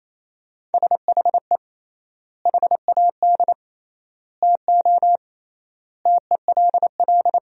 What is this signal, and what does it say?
Morse code, heard as a keyed tone, about 28 wpm, 700 Hz. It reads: SHE HAD TO TELL
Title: 3x Word-Spacing - Sentences from Top 200 Words - 28wpm